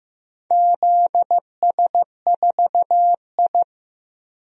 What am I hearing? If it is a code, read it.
Morse code, heard as a keyed tone, about 15 wpm, 700 Hz. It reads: ZS4I
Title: DXCC-100 Call Signs - 4 Characters - Repeated 2x - 15wpm